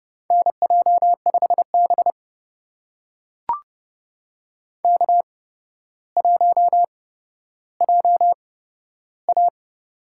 Morse code, heard as a keyed tone, about 31 wpm, spaced as for 13 wpm, 700 Hz. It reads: NJ5B K1JA
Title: USA Call Signs - 4 Characters - 8x Character Spacing - 30wpm